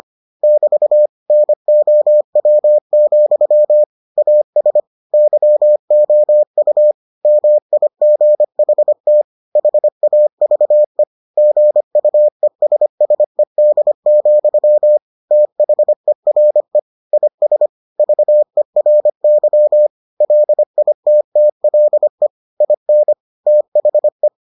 Morse code, heard as a keyed tone, about 25 wpm, 600 Hz. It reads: <BT> NOW, AS YOU MIGHT HAVE GUESSED, THERE IS VERY LITTLE IN THE